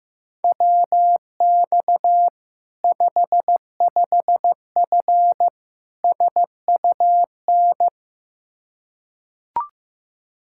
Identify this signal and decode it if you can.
Morse code, heard as a keyed tone, about 15 wpm, 700 Hz. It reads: WX 55F SUN
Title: K1USN SST QSO Practice 15wpm